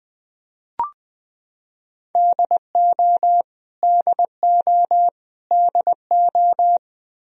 Morse code, heard as a keyed tone, about 20 wpm, 700 Hz. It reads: DO DO DO
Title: Top 100 Words Repeated 20wpm